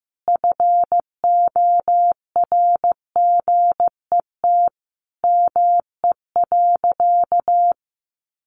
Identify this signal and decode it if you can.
Morse code, heard as a keyed tone, about 15 wpm, 700 Hz. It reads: FORGET ME.
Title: Sentences from Top 1200 Words 15wpm